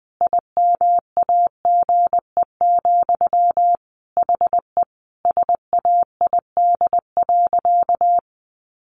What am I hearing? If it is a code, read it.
Morse code, heard as a keyed tone, about 19 wpm, 700 Hz. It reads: IMAGE, HE SAID.
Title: Sentences from Top 700 Words 20wpm